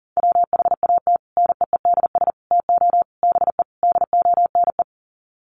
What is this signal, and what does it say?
Morse code, heard as a keyed tone, about 37 wpm, 700 Hz. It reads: WHAT NEEDS TO BE DONE
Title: Sets of 5 Words Encore 40wpm